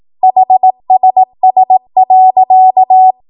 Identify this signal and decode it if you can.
Morse code, heard as a keyed tone, about 18 wpm, 750 Hz. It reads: HSS.
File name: Morse / 18WPM